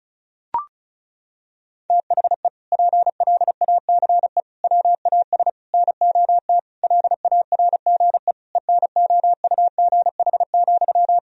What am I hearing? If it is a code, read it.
Morse code, heard as a keyed tone, about 34 wpm, 700 Hz. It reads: THE PLACE WAS NOT LARGE ENOUGH,
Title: Sentences from Top 1000 Words - Review Part 1 - 35wpm